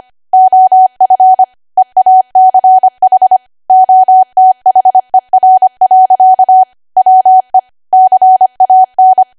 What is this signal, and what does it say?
Morse code, heard as a keyed tone, about 25 wpm, 750 Hz. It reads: OF EACH OTHER. WE CAN
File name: Archive / 25 WPM / 25WPM